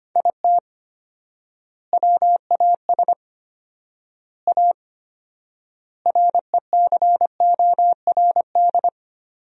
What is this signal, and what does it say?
Morse code, heard as a keyed tone, about 25 wpm, 700 Hz. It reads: IT WAS A RECORD
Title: 4x Word-Spacing - Sentences from Top 500 Words - 25wpm